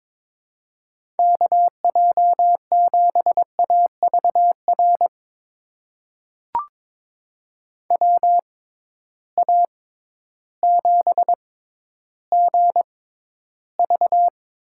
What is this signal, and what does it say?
Morse code, heard as a keyed tone, about 22 wpm, spaced as for 12 wpm, 700 Hz. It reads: KJ7AVR WA7GV
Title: USA Call Signs - 6x Character Spacing - 22wpm